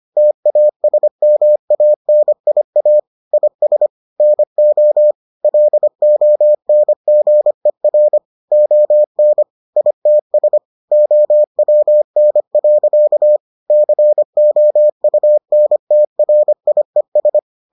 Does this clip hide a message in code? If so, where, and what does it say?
Yes: Morse code from 0.2 to 17.4 s, heard as a keyed tone, about 25 wpm, 600 Hz. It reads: TASMANIA IS NO LONGER ON ITS OWN. COUNTRIES